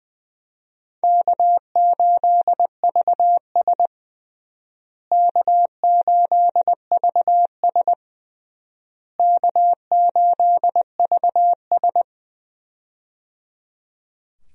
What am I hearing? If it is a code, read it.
Morse code, heard as a keyed tone, about 20 wpm, 700 Hz. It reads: K8VS K8VS K8VS
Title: USA Call Signs - 4 Characters - Repeated 3x - 20wpm